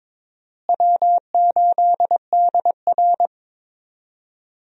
Morse code, heard as a keyed tone, about 22 wpm, 700 Hz. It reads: W8DR
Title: USA Call Signs - 4 Characters - Repeated 2x - 22wpm